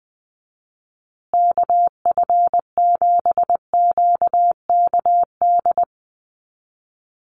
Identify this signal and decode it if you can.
Morse code, heard as a keyed tone, about 20 wpm, 700 Hz. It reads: KF7QKD
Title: USA Call Signs Repeated 2x 20wpm